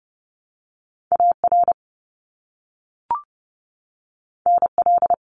Morse code, heard as a keyed tone, about 30 wpm, 700 Hz. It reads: AR NL
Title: ARRL Sections 30wpm